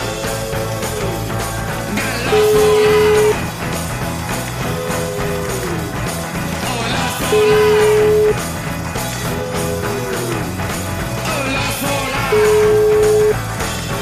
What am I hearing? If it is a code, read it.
Ringback tone (Europe, 425 Hz), with noise.